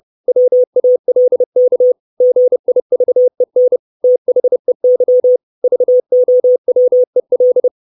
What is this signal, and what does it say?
Morse code, heard as a keyed tone, about 30 wpm, 500 Hz. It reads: WALK GIVEN THEY VOWEL
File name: 632 Words 30WPM